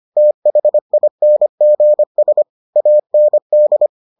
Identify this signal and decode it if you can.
Morse code, heard as a keyed tone, about 25 wpm, 600 Hz. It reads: THINGS AND